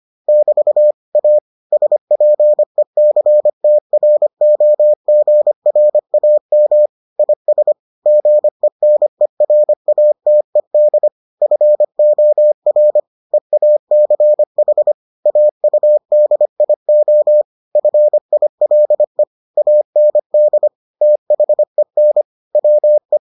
Morse code, heard as a keyed tone, about 25 wpm, 600 Hz. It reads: <BT> A SPECTROGRAM IS GENERATED FOR EACH AUDIO FILE AND THEN WE